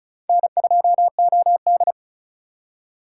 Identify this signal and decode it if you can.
Morse code, heard as a keyed tone, about 36 wpm, 700 Hz. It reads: N2OD
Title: USA Call Signs - 4 Characters - 4x Character Spacing - 35wpm